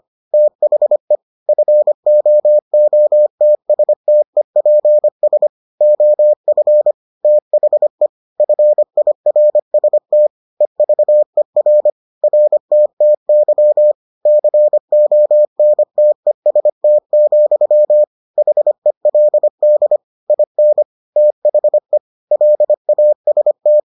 Morse code, heard as a keyed tone, about 25 wpm, 600 Hz. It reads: THE FOOTSTEPS OF THE FIRST EVER RTTY CONTEST, HELD IN THE LAST